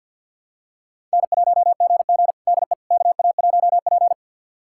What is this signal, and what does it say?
Morse code, heard as a keyed tone, about 46 wpm, 700 Hz. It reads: N1GG DE KA1P
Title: USA Callsign DE Callsign - 4 Characters - 50wpm